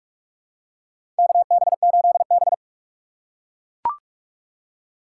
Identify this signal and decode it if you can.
Morse code, heard as a keyed tone, about 43 wpm, 700 Hz. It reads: KB8B
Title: USA Call Signs - 4 Characters - 2x Character Spacing - 45wpm